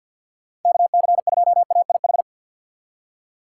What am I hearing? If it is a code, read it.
Morse code, heard as a keyed tone, about 51 wpm, 700 Hz. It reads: KC2AIH
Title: USA Call Signs - 6 Characters - 3x Character Spacing - 50wpm